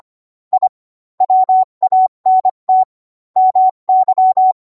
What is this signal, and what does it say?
Morse code, heard as a keyed tone, about 25 wpm, 750 Hz. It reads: I WANT MY